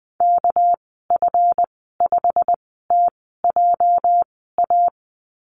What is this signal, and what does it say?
Morse code, heard as a keyed tone, about 19 wpm, spaced as for 14 wpm, 700 Hz. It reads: KF5TJA
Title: 238 Lesson 38g - Letter q - Call Signs 20wpm